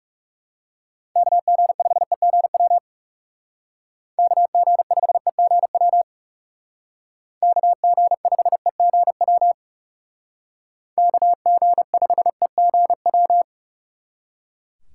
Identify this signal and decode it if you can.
Morse code, heard as a keyed tone, about 37 wpm, 700 Hz. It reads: KG5EGW KG5EGW KG5EGW KG5EGW
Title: Speed Racing - USA 2x3 Call Signs 30wpm